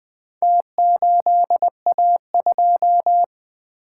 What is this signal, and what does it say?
Morse code, heard as a keyed tone, about 20 wpm, 700 Hz. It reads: T8A2
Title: Four Letter-Number Combinations - 20wpm